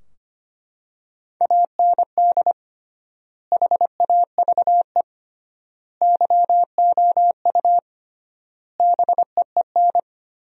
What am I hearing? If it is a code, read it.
Morse code, heard as a keyed tone, about 25 wpm, 700 Hz. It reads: AND HAVE YOU BEEN
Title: 3x Word-Spacing - Sentences from Top 200 Words - 25wpm